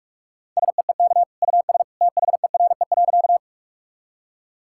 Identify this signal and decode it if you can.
Morse code, heard as a keyed tone, about 46 wpm, 700 Hz. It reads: SEEK US THERE.
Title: Sentences from Top 1600 Words 45wpm